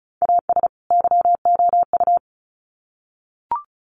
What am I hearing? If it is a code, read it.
Morse code, heard as a keyed tone, about 34 wpm, 700 Hz. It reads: AS YOU
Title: Sets of 2 Words 35wpm